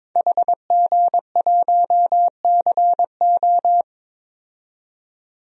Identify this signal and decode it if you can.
Morse code, heard as a keyed tone, about 22 wpm, 700 Hz. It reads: HG1CO
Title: DXCC-100 Call Signs - 5 Characters - 22wpm